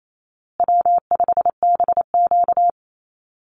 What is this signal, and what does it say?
Morse code, heard as a keyed tone, about 28 wpm, 700 Hz. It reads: W5BQ
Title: Easy Call Signs Repeated 3x - 28wpm